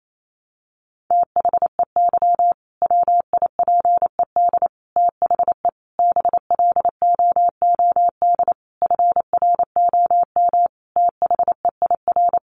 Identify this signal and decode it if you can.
Morse code, heard as a keyed tone, about 28 wpm, 700 Hz. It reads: THEY WIPED THE BLOOD FROM THEIR
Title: A Sound Of Thunder 28wpm